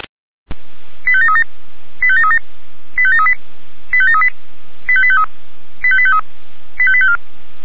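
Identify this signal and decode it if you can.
Five-tone selective call (ZVEI), heard as quick runs of five tones, tones of about 80 ms. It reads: 85737 85737 85738 85738 85853 85853 85854
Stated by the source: ZVEI five-tone selective call received with an SDR.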